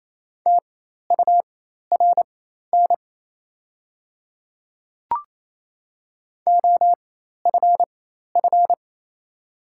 Morse code, heard as a keyed tone, about 28 wpm, spaced as for 13 wpm, 700 Hz. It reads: TURN OFF
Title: 4x Character Spacing - Top 300 Words - 28wpm